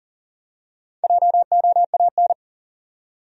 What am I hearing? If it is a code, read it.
Morse code, heard as a keyed tone, about 40 wpm, 700 Hz. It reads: JOAN
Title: USA Common Names - 40wpm